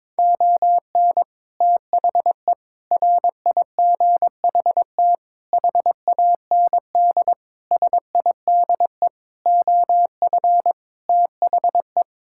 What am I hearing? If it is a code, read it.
Morse code, heard as a keyed tone, about 22 wpm, 700 Hz. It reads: ON THE RIGHT HAND SIDE OF THE
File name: practice / 22wpm / Sets of 7 Words - 22wpm